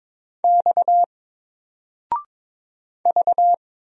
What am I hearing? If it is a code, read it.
Morse code, heard as a keyed tone, about 22 wpm, 700 Hz. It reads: X V
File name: Single-Letter-Number-22wpm